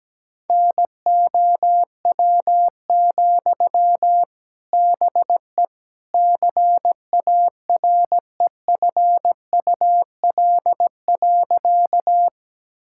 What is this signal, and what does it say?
Morse code, heard as a keyed tone, about 17 wpm, 700 Hz. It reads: NOW, BE CAREFUL.
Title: Sentences from Top 1600 Words 17wpm